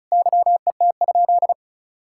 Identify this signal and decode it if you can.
Morse code, heard as a keyed tone, about 36 wpm, 700 Hz. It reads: YET?
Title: Sentences from Top 1000 Words - Review Part 6 - 35wpm